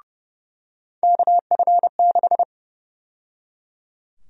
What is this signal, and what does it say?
Morse code, heard as a keyed tone, about 30 wpm, 700 Hz. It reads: KF6
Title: USA Call Sign Prefixes - 30wpm